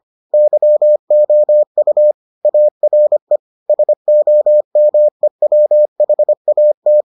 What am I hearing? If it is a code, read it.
Morse code, heard as a keyed tone, about 25 wpm, 600 Hz. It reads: YOU ARE SOMEWHAT